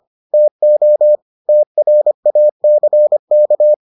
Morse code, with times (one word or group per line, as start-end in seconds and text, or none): 0.3-1.2 TO
1.5-3.7 TRACK